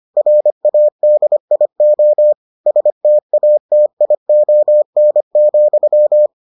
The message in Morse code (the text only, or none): RADIO STATION,